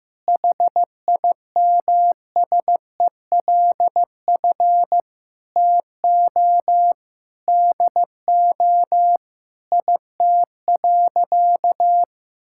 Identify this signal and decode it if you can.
Morse code, heard as a keyed tone, about 15 wpm, 700 Hz. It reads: HIMSELF TO DO IT.
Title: Sentences from Top 1600 Words 15wpm